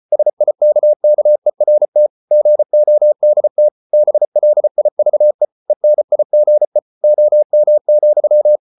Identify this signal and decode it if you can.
Morse code, heard as a keyed tone, about 34 wpm, 600 Hz. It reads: SIKKERT GODT BLIVE ENIGE OM,
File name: Ferie QRQ 34wpm